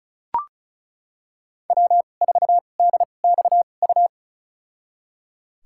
Morse code, heard as a keyed tone, about 36 wpm, spaced as for 25 wpm, 700 Hz. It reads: W4DXU
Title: USA Call Signs - 5 Characters - 2x Character Spacing - 35wpm